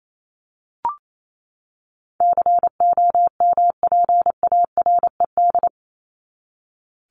Morse code, heard as a keyed tone, about 28 wpm, 700 Hz. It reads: COMPARED
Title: ED Suffix - Top 100 Words with Base Words 28wpm